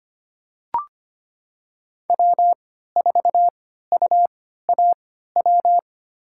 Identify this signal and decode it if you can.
Morse code, heard as a keyed tone, about 25 wpm, spaced as for 14 wpm, 700 Hz. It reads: W4UAW
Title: USA Call Signs - 5 Characters - 3x Character Spacing - 25wpm